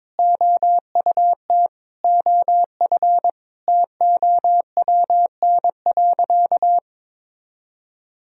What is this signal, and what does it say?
Morse code, heard as a keyed tone, about 22 wpm, 700 Hz. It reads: OUT OF TOWN.
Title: Sentences from Top 1000 Words - Review Part 4 - 22wpm